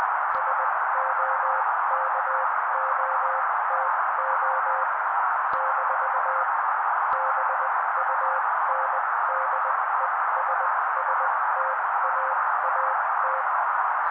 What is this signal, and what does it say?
Morse code, heard as a keyed tone, about 20 wpm, spaced as for 16 wpm, 530 Hz. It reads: SOKOTO - BUNDESSTAAT
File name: daily / 16wpm noise